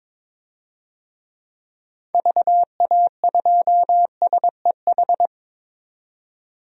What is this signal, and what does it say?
Morse code, heard as a keyed tone, about 22 wpm, 700 Hz. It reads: VA2SEH
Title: DXCC-100 Call Signs - 6 Characters - 22wpm